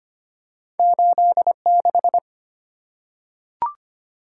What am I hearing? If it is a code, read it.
Morse code, heard as a keyed tone, about 25 wpm, 700 Hz. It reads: 86
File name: Two Letter-Number Combinations - 25wpm